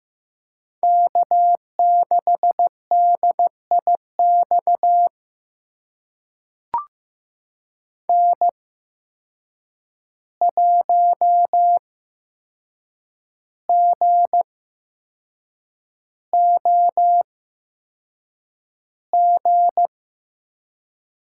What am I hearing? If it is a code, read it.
Morse code, heard as a keyed tone, about 15 wpm, spaced as for 6 wpm, 700 Hz. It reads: K6DIX N1GOG
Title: USA Call Signs - 5 Characters - 8x Character Spacing - 15wpm